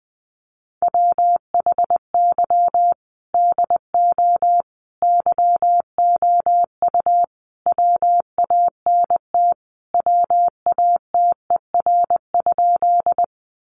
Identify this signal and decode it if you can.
Morse code, heard as a keyed tone, about 19 wpm, 700 Hz. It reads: WHY DO YOU WANT WATER?
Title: Sentences from Top 200 Words 20wpm